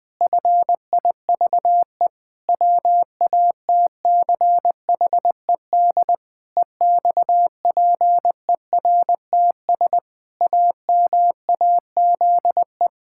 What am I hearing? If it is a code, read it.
Morse code, heard as a keyed tone, about 20 wpm, 700 Hz. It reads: FIVE WATCHED EXPERTS AMAZE
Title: Pangram Challenge v2 - 20wpm